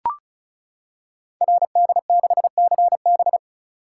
Morse code, heard as a keyed tone, about 36 wpm, 690 Hz. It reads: RD6CB
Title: Code Groups - 5 Letter:Numbers - 35wpm